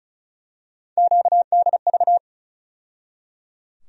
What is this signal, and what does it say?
Morse code, heard as a keyed tone, about 34 wpm, 700 Hz. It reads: QDV
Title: USA Call Sign Postfixes - 3 Letters 35wpm